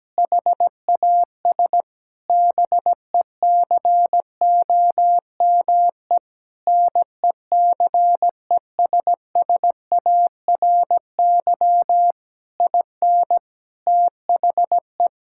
Morse code, heard as a keyed tone, about 17 wpm, 700 Hz. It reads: HAS BECOME NECESSARY IN THE
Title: Sentences from Top 1000 Words - Review Part 5 - 17wpm